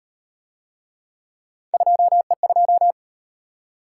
Morse code, heard as a keyed tone, about 38 wpm, 700 Hz. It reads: 2E2